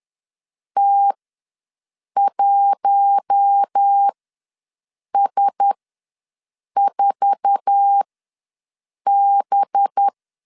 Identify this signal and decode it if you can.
Morse code, heard as a keyed tone, about 10 wpm, spaced as for 6 wpm, 800 Hz. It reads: T1S4B
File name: LEZIONI / 9 28 ws 10wpm 4124